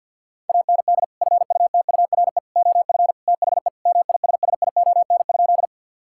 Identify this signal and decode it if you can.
Morse code, heard as a keyed tone, about 50 wpm, 700 Hz. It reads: AND FUTURE OF THE MISSION?
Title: Sentences from Top 1600 Words 50wpm